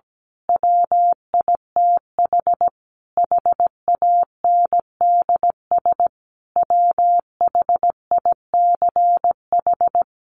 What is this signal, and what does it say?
Morse code, heard as a keyed tone, about 17 wpm, 700 Hz. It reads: WITH HANDS WHICH
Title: A Sound Of Thunder 17wpm